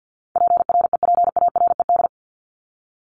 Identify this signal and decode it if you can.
Morse code, heard as a keyed tone, about 50 wpm, 700 Hz. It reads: PREPARED
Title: ED Suffix - Top 100 Words with Base Words 50wpm